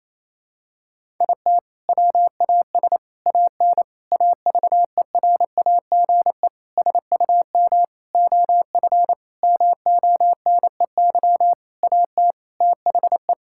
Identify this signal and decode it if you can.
Morse code, heard as a keyed tone, about 28 wpm, 700 Hz. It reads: IT WAS AN AVERAGE SUM OF MONEY AT THE
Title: Sentences from Top 1000 Words - Review Part 6 - 28wpm